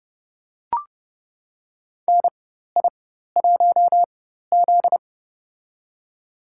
Morse code, heard as a keyed tone, about 32 wpm, spaced as for 14 wpm, 700 Hz. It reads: NI1Z
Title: USA Call Signs - 4 Characters - 4x Character Spacing - 30wpm